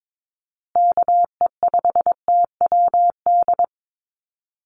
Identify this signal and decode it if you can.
Morse code, heard as a keyed tone, about 22 wpm, 700 Hz. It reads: KE5TWD
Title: USA Call Signs Repeated 3x 22wpm